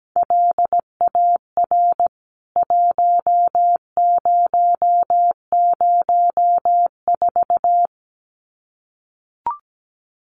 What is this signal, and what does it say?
Morse code, heard as a keyed tone, about 17 wpm, 700 Hz. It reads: LAR 1004
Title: CWT Exchange 17wpm